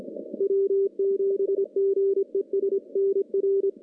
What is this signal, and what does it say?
Morse code, heard as a keyed tone, about 26 wpm, 390 Hz. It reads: W7GES NR
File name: contests / CW-SS-k5tr- -1300